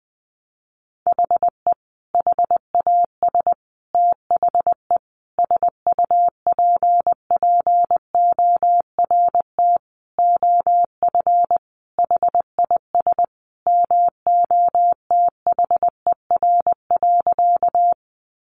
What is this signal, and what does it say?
Morse code, heard as a keyed tone, about 20 wpm, 700 Hz. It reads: HE HAS THE SUPPORT OF HIS MOTHER.
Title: Sentences from Top 800 Words 20wpm